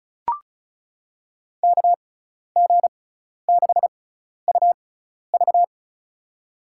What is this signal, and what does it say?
Morse code, heard as a keyed tone, about 34 wpm, spaced as for 12 wpm, 700 Hz. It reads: KG6UV
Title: USA Call Signs - 5 Characters - 6x Character Spacing - 35wpm